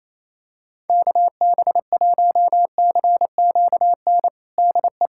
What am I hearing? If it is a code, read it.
Morse code, heard as a keyed tone, about 28 wpm, 700 Hz. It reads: KB1CQN DE
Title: USA Callsign DE Callsign 28wpm